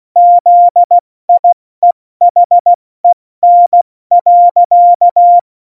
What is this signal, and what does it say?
Morse code, heard as a keyed tone, about 16 wpm, 700 Hz. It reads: ZIEHEN.
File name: Morsekurs / Witze / Witz 20. November 2018, 70 F80